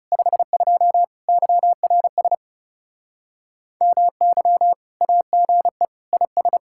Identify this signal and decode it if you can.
Morse code, heard as a keyed tone, about 40 wpm, 700 Hz. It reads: 52 YRS MY AGE IS